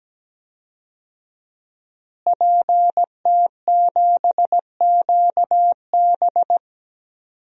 Morse code, heard as a keyed tone, about 17 wpm, 700 Hz. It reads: PT7QB
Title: DXCC-100 Call Signs - 5 Characters - Repeated 3x - 17wpm